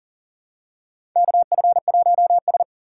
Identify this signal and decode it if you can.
Morse code, heard as a keyed tone, about 40 wpm, 700 Hz. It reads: KF1S
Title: USA Call Signs - 4 Characters - 4x Character Spacing - 40wpm